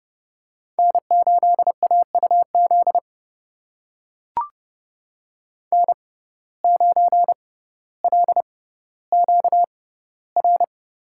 Morse code, heard as a keyed tone, about 30 wpm, spaced as for 15 wpm, 700 Hz. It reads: N8AUZ N9LQR
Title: USA Call Signs - 5 Characters - 6x Character Spacing - 30wpm